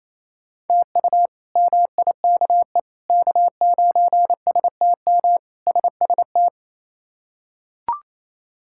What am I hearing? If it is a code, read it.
Morse code, heard as a keyed tone, about 28 wpm, 700 Hz. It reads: TU MIKE K9STM SST
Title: K1USN SST QSO Practice - 28wpm